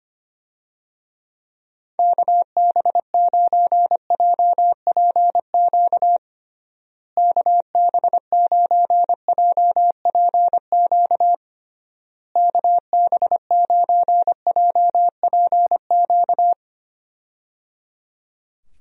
Morse code, heard as a keyed tone, about 25 wpm, 700 Hz. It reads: KB9JPQ KB9JPQ KB9JPQ
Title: USA Call Signs - 6 Characters - Repeated 3x - 25wpm